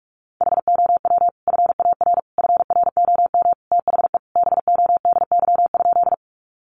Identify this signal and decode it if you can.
Morse code, heard as a keyed tone, about 44 wpm, 700 Hz. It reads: HOW FAR FROM THE BODY?
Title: Sentences from Top 400 Words 45wpm